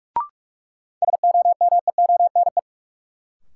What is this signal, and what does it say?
Morse code, heard as a keyed tone, about 46 wpm, 700 Hz. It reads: SOMEONE
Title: Sentences from Top 600 Words - New Word Drill - 45wpm